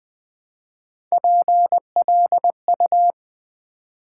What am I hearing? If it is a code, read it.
Morse code, heard as a keyed tone, about 20 wpm, 700 Hz. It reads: PLU
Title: Build A Word - 1 Letter At A Time - 4-Letter Words - 20wpm